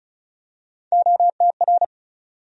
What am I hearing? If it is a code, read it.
Morse code, heard as a keyed tone, about 34 wpm, 700 Hz. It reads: OTR
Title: USA Call Sign Postfixes - 3 Letters 35wpm